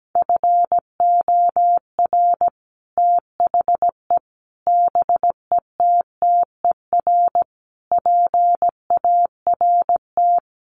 Morse code, heard as a keyed tone, about 17 wpm, 700 Hz. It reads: FOR THE BETTER PART
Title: Sets of 5 Words Encore2 17wpm